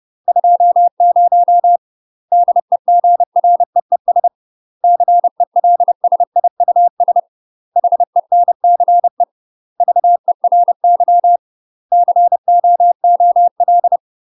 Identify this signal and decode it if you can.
Morse code, heard as a keyed tone, about 30 wpm, 700 Hz. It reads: 20 DEGREES CELSIUS HENCE VERY COOL
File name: UK MASTER QSOs TRAINING / 30WPM / UK MASTER QSOs 0022